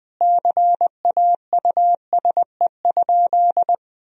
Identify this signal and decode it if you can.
Morse code, heard as a keyed tone, about 21 wpm, 700 Hz. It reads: CAUSE?
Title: Sentences from Top 1000 Words - Review Part 1 - 20wpm